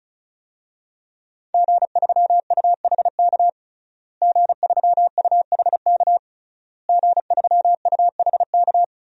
Morse code, heard as a keyed tone, about 34 wpm, 700 Hz. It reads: G3UHK G3UHK G3UHK
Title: DXCC-100 Call Signs - 5 Characters - Repeated 3x - 35wpm